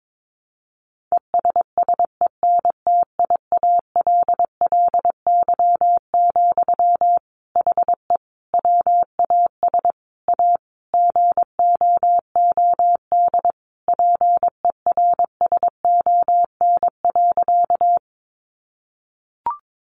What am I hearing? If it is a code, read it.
Morse code, heard as a keyed tone, about 22 wpm, 700 Hz. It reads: ESSENTIALLY, HE WAS A GOOD PERSON.
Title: Sentences from Top 1300 Words 22wpm